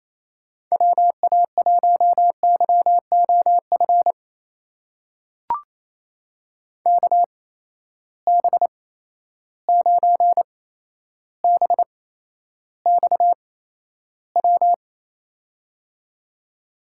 Morse code, heard as a keyed tone, about 28 wpm, spaced as for 12 wpm, 700 Hz. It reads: WA1YOF KB9BXW
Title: USA Call Signs - 6 Characters - 8x Character Spacing - 28wpm